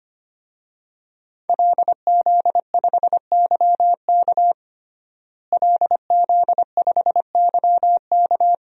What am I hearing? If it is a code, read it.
Morse code, heard as a keyed tone, about 25 wpm, 700 Hz. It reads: LZ5YK LZ5YK
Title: DXCC-100 Call Signs - 5 Characters - Repeated 2x - 25wpm